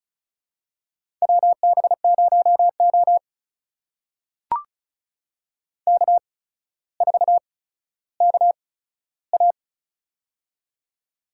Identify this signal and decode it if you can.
Morse code, heard as a keyed tone, about 36 wpm, spaced as for 15 wpm, 700 Hz. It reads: WB0O K4KA
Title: USA Call Signs - 4 Characters - 8x Character Spacing - 35wpm